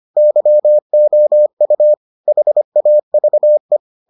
Morse code, heard as a keyed tone, about 25 wpm, 600 Hz. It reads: YOU HAVE